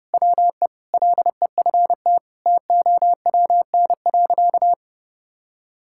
Morse code, heard as a keyed tone, about 28 wpm, 700 Hz. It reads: WE LEFT TOWN.
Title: Sentences from Top 1000 Words - Review Part 6 - 30wpm